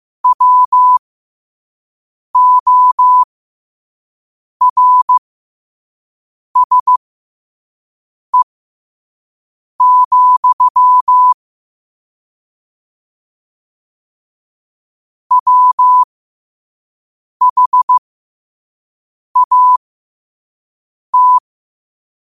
Morse code, heard as a keyed tone, about 15 wpm, spaced as for 5 wpm, 1000 Hz. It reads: WORSE, WHAT
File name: qotdfiles / 5wpm